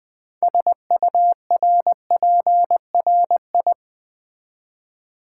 Morse code, heard as a keyed tone, about 20 wpm, 690 Hz. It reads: SURPRI
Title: Build A Word - 2 Letters At A Time - 8-Letter Words - 20wpm